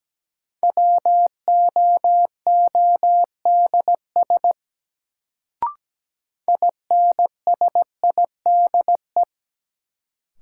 Morse code, heard as a keyed tone, about 17 wpm, 700 Hz. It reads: WOODS INSIDE
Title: Sentences from Top 1000 Words - Review Part 2 - New Word Drill - 17wpm